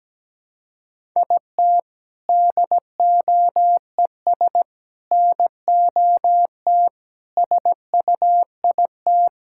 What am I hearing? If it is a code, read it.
Morse code, heard as a keyed tone, about 17 wpm, 700 Hz. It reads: IT DOES NOT SUIT
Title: Sentences from Top 1600 Words 17wpm